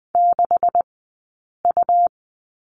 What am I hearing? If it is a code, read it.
Morse code, heard as a keyed tone, about 20 wpm, spaced as for 8 wpm, 700 Hz. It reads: 6U
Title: 204 Lesson 33d - Letter j - Two Characters 20wpm